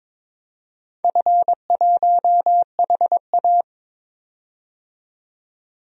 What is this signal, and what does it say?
Morse code, heard as a keyed tone, about 22 wpm, 700 Hz. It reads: F1HA
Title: DXCC-100 Call Signs - 4 Characters - 22wpm